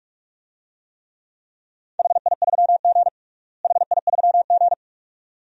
Morse code, heard as a keyed tone, about 46 wpm, 700 Hz. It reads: HI3G HI3G
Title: DXCC-100 Call Signs - 4 Characters - Repeated 2x - 45wpm